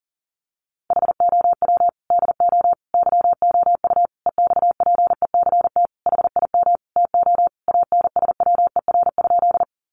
Morse code, heard as a keyed tone, about 37 wpm, 700 Hz. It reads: HOW DO YOU EXPECT HIM TO ANSWER?
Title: Sentences from Top 1100 Words 40wpm